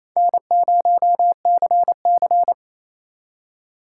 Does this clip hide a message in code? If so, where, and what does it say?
Yes: Morse code from 0.2 to 2.5 s, heard as a keyed tone, about 35 wpm, 700 Hz. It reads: N0CC